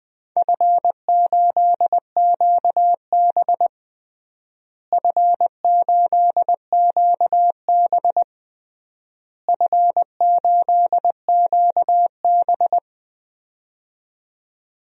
Morse code, heard as a keyed tone, about 20 wpm, 700 Hz. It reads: F8QB F8QB F8QB
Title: DXCC-100 Call Signs - 4 Characters - Repeated 3x - 20wpm